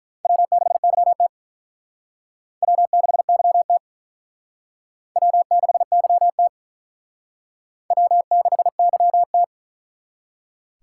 Morse code, heard as a keyed tone, about 41 wpm, 700 Hz. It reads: W6YT W6YT W6YT W6YT
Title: Speed Racing - USA 1x2 Call Signs 35wpm